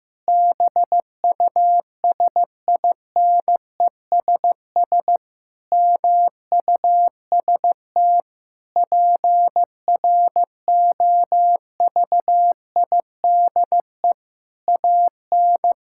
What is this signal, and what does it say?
Morse code, heard as a keyed tone, about 15 wpm, 700 Hz. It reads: BUSINESS MUST PROVIDE AN
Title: Sentences from Top 1000 Words - Review Part 5 - 15wpm